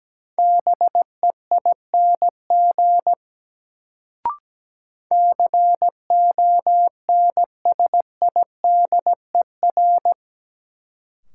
Morse code, heard as a keyed tone, about 17 wpm, 700 Hz. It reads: BEING CONSIDER
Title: Sentences from Top 600 Words - New Word Drill - 17wpm